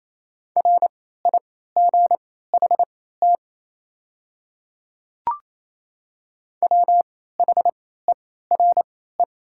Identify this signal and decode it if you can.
Morse code, heard as a keyed tone, about 27 wpm, spaced as for 16 wpm, 700 Hz. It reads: RIGHT WHERE
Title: 3x Character Spacing - Top 200 Words - 28wpm